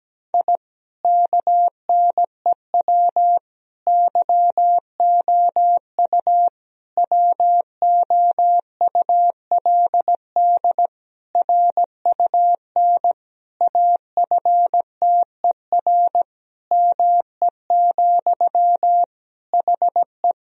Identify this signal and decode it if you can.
Morse code, heard as a keyed tone, about 17 wpm, 700 Hz. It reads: I KNEW YOU WOULD RUN AFTER ME, HE
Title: Sentences from Top 1000 Words - Review Part 1 - 17wpm